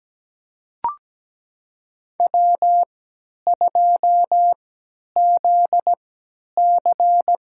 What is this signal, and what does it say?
Morse code, heard as a keyed tone, about 17 wpm, spaced as for 10 wpm, 700 Hz. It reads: W2ZC
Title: USA Call Signs - 3x Character Spacing - 17wpm